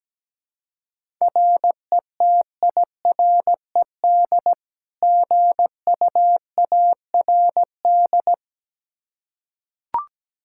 Morse code, heard as a keyed tone, about 17 wpm, 700 Hz. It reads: RETIRED GUARD
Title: K1USN SST QSO Practice 17wpm